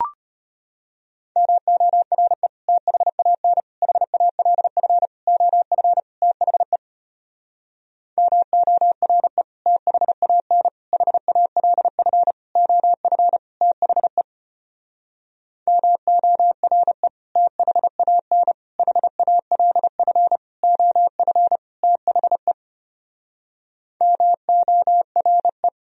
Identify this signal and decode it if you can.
Morse code, heard as a keyed tone, about 33 wpm, 700 Hz. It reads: MORE THAN HALF OF THE MORE THAN HALF OF THE MORE THAN HALF OF THE MORE